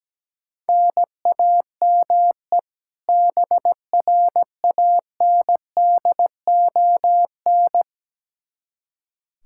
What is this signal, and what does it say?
Morse code, heard as a keyed tone, about 17 wpm, 700 Hz. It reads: NAME BRANDON
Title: K1USN SST QSO Practice 17wpm